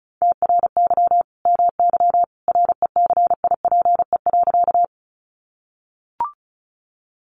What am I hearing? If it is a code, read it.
Morse code, heard as a keyed tone, about 34 wpm, 700 Hz. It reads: TRY MY RECIPE.
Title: Sentences from Top 1200 Words 35wpm